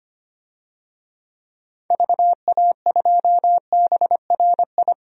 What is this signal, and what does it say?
Morse code, heard as a keyed tone, about 25 wpm, 700 Hz. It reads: VA2BRI
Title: DXCC-100 Call Signs - 6 Characters - 25wpm